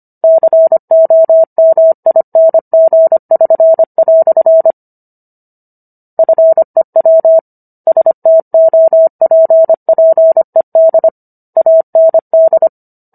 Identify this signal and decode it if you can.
Morse code, heard as a keyed tone, about 25 wpm, 650 Hz. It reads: COMING<SN>" FEW STOPPED AND